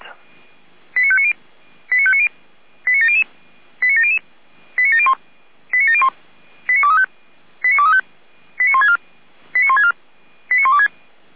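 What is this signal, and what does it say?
Five-tone selective call (ZVEI), heard as quick runs of five tones, tones of about 70 ms. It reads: 89590 89590 89700 89700 89701 89701 89235 89235 89164 89164 89126
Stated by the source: ZVEI five-tone selective call received on an SDR.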